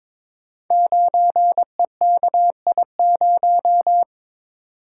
Morse code, heard as a keyed tone, about 22 wpm, 700 Hz. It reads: 9EKI0
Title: Code Groups - 5 Letter:Numbers - 22wpm